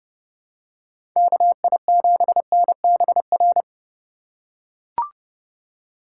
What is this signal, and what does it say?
Morse code, heard as a keyed tone, about 32 wpm, 700 Hz. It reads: KI7NBR
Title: USA Call Signs - 6 Characters - 2x Character Spacing - 30wpm